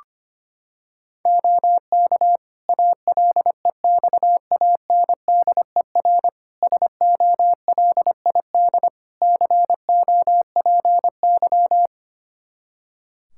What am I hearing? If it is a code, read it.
Morse code, heard as a keyed tone, about 25 wpm, 700 Hz. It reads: OK ALEXANDER SOLID COPY
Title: K1USN SST QSO Practice 25wpm